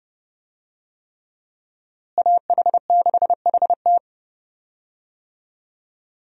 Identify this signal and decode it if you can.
Morse code, heard as a keyed tone, about 32 wpm, 700 Hz. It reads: AH6HT
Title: USA Call Signs - 5 Characters - Repeated 3x - 30wpm